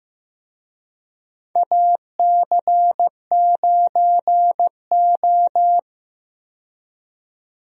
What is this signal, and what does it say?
Morse code, heard as a keyed tone, about 15 wpm, 700 Hz. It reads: AC9O
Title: USA Call Signs - 4 Characters - 15wpm